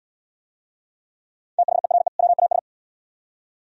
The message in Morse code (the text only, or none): THERE WAS